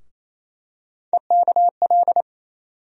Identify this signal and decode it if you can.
Morse code, heard as a keyed tone, about 28 wpm, 700 Hz. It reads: EKL
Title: USA Call Sign Postfixes - 3 Letters 28wpm